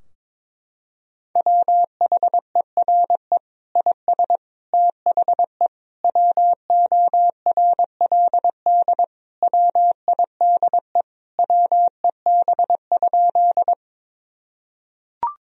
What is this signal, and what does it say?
Morse code, heard as a keyed tone, about 22 wpm, 700 Hz. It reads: WHERE IS THE WORLD WIDE WEB?
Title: Sentences from Top 1000 Words - Review Part 4 - 22wpm